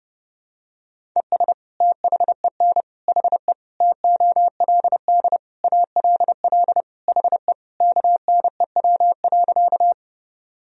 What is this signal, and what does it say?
Morse code, heard as a keyed tone, about 32 wpm, 700 Hz. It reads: ES THEN HE TOLD ALL HE KNEW.
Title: Sentences from Top 400 Words and Abbreviations 30wpm